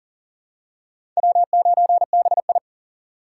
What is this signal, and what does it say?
Morse code, heard as a keyed tone, about 40 wpm, 700 Hz. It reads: W9BI
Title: USA Call Signs - 4x Character Spacing - 40wpm